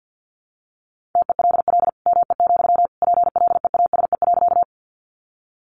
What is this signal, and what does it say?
Morse code, heard as a keyed tone, about 50 wpm, 700 Hz. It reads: TELL ME, PLEASE.
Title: Sentences from Top 1500 Words 50wpm